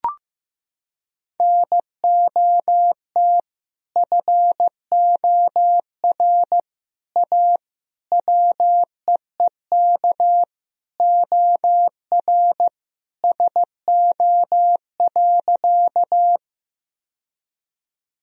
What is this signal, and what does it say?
Morse code, heard as a keyed tone, about 15 wpm, 700 Hz. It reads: NOT FOR A WEEK OR SO.
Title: Sentences from Top 1000 Words - Review Part 2 - 15wpm